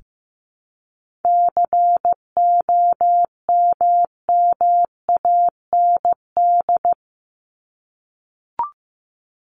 Morse code, heard as a keyed tone, about 15 wpm, 700 Hz. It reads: COMMAND
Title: Top 1501-1750 Words 15wpm